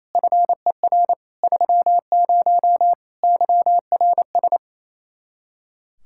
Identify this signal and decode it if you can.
Morse code, heard as a keyed tone, about 27 wpm, 700 Hz. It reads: FER 30 YRS
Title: Practice QSO - Exchange 3 - More Info - 28wpm